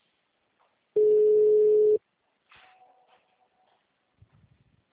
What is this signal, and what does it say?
Ringback tone (Europe, 425 Hz).